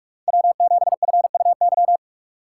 Morse code, heard as a keyed tone, about 45 wpm, 700 Hz. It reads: W7FUY